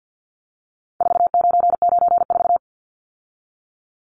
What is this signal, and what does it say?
Morse code, heard as a keyed tone, about 48 wpm, 700 Hz. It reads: 4994
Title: Numbers - 4-Digit 50wpm